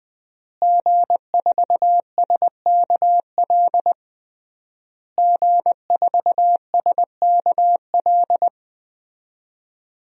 Morse code, heard as a keyed tone, about 20 wpm, 700 Hz. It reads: G4SKL G4SKL
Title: DXCC-100 Call Signs - 5 Characters - Repeated 3x - 20wpm